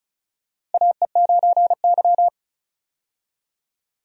Morse code, heard as a keyed tone, about 34 wpm, 700 Hz. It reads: AE9Y
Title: USA Call Signs - 4 Characters - Repeated 2x - 35wpm